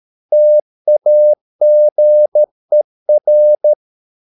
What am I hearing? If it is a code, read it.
Morse code, heard as a keyed tone, about 13 wpm, 600 Hz. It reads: TAGER
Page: Ferie 13 wpm